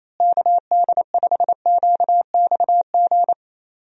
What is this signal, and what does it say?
Morse code, heard as a keyed tone, about 28 wpm, 700 Hz. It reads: KD5QXG
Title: USA Call Signs - 6 Characters - Repeated 3x - 28wpm